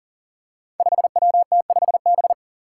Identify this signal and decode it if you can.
Morse code, heard as a keyed tone, about 40 wpm, 700 Hz. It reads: 5WT5B
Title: Five Challenging Letter-Number Combinations 40wpm